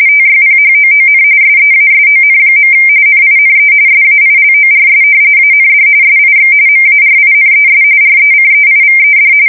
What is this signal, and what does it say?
RTTY, about 45.45 baud. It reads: TYPHOON SEASON. NINE SAILORS AND SIX MARINES RECEIVED
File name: fsk170 / usnavy3